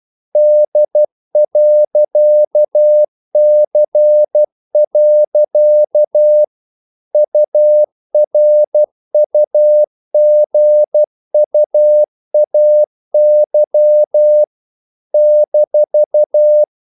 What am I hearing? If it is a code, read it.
Morse code, heard as a keyed tone, about 12 wpm, 600 Hz. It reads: D.C. URUGUAY -
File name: CW / Practice / NationalCapitals12